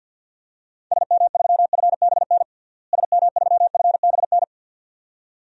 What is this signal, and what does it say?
Morse code, heard as a keyed tone, about 48 wpm, 700 Hz. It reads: SM3FBN SM3FBN
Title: DXCC-100 Call Signs - 6 Characters - Repeated 2x - 50wpm